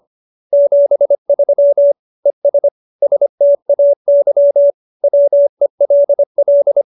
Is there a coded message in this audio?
Yes: Morse code from 0.5 to 6.8 s, heard as a keyed tone, about 25 wpm, 570 Hz. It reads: 73 ES STAY WELL